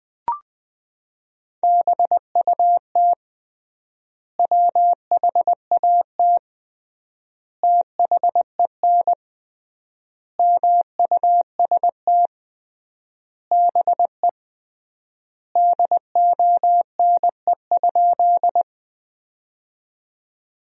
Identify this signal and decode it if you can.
Morse code, heard as a keyed tone, about 21 wpm, 700 Hz. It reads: BUT WHAT THEN MUST BE DONE?
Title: 3x Word-Spacing - Sentences from Top 500 Words - 20wpm